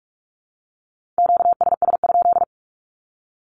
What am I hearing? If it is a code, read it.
Morse code, heard as a keyed tone, about 45 wpm, 700 Hz. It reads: QSS?
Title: Q-Codes Repeated 3x 45wpm